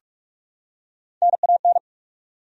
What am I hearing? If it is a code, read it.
Morse code, heard as a keyed tone, about 46 wpm, 700 Hz. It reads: NAN
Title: USA Common Names - Max 3 Letters - 45wpm